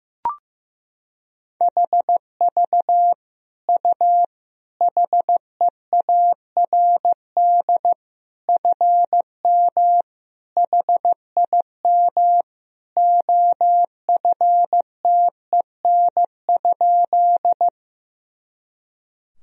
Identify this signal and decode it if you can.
Morse code, heard as a keyed tone, about 15 wpm, 700 Hz. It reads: HV U HEARD FM HIM OFTEN?
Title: Sentences from Top 400 Words and Abbreviations 15wpm